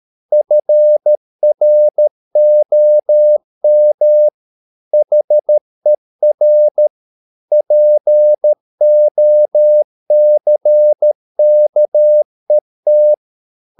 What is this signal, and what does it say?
Morse code, heard as a keyed tone, about 13 wpm, 600 Hz. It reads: FROM HER POCKET